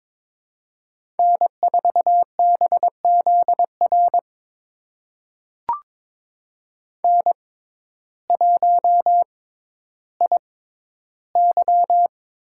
Morse code, heard as a keyed tone, about 22 wpm, spaced as for 12 wpm, 700 Hz. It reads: N4BZR N1IY
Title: USA Call Signs - 5 Characters - 6x Character Spacing - 22wpm